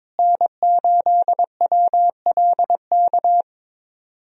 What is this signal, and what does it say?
Morse code, heard as a keyed tone, about 22 wpm, 700 Hz. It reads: N8WLK
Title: USA Call Signs - 5 Characters - Repeated 3x - 22wpm